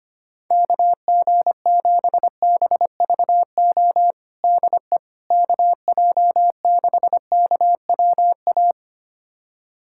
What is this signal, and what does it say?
Morse code, heard as a keyed tone, about 25 wpm, 700 Hz. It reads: KG7BVO DE KJ6KWA
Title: USA Callsign DE Callsign - 6 Characters - 25wpm